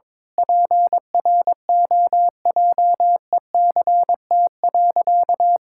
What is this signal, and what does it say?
Morse code, heard as a keyed tone, about 22 wpm, 700 Hz. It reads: PROJECT.
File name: Sentences with 7-Word Phrases - 22wpm